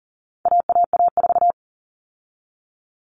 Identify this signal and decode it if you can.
Morse code, heard as a keyed tone, about 40 wpm, 700 Hz. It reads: AAA4
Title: All Cut Numbers - 4-Digit 40wpm